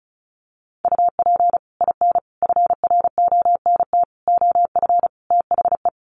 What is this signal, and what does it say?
Morse code, heard as a keyed tone, about 34 wpm, 700 Hz. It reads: UP IN FRONT OF THE
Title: Sets of 5 Words Encore2 35wpm